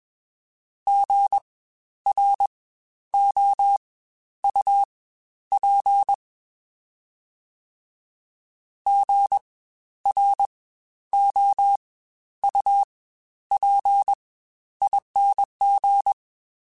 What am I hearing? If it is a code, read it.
Morse code, heard as a keyed tone, about 21 wpm, spaced as for 11 wpm, 800 Hz. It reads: GROUP GROUPING
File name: Spaced Alphas